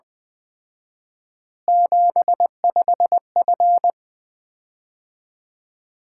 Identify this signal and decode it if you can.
Morse code, heard as a keyed tone, about 20 wpm, 700 Hz. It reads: 75F
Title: Common QSO Phrases - 4x Word-Spacing - 20wpm